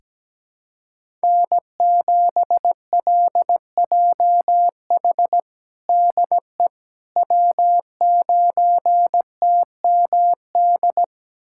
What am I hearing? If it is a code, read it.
Morse code, heard as a keyed tone, about 17 wpm, 700 Hz. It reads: N7LJH DE W9TMD
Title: USA Callsign DE Callsign - 5 Characters - 17wpm